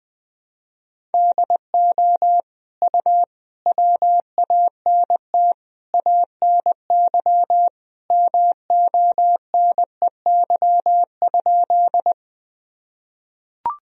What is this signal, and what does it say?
Morse code, heard as a keyed tone, about 20 wpm, 700 Hz. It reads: DO U WANT ANY MONEY?
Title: Sentences from Top 400 Words and Abbreviations 20wpm